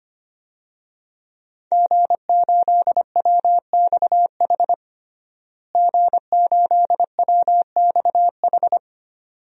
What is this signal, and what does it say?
Morse code, heard as a keyed tone, about 25 wpm, 700 Hz. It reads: G8WXH G8WXH
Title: DXCC-100 Call Signs - 5 Characters - Repeated 2x - 25wpm